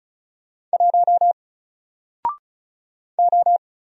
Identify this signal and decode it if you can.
Morse code, heard as a keyed tone, about 35 wpm, 700 Hz. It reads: 1 O
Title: Single Character (Letter:Number:Punctuation:Prosign) 35wpm